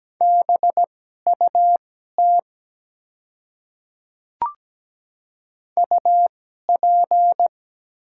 Morse code, heard as a keyed tone, about 17 wpm, spaced as for 12 wpm, 700 Hz. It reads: BUT UP
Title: 2x Character Spacing - Top 100 Words - 17wpm